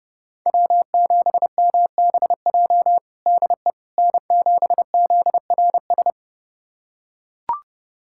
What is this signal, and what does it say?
Morse code, heard as a keyed tone, about 32 wpm, 700 Hz. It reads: W7MBJ DE N7ZRS
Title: USA Callsign DE Callsign - 5 Characters - 30wpm